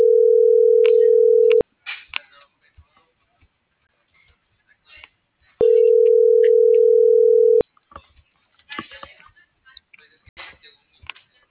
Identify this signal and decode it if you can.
Ringback tone (North America), 440 + 480 Hz.